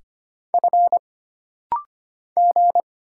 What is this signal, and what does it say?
Morse code, heard as a keyed tone, about 25 wpm, 700 Hz. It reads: F G
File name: Single Character (Letter:Number:Punctuation:Prosign) - Rapid-Fire - 25wpm